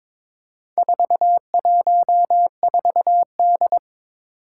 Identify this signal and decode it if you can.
Morse code, heard as a keyed tone, about 23 wpm, 700 Hz. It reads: 414D
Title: Four Challenging Letter-Number Combinations 22wpm